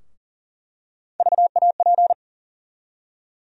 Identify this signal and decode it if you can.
Morse code, heard as a keyed tone, about 40 wpm, 700 Hz. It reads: VAP
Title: USA Call Sign Postfixes - 3 Letters 40wpm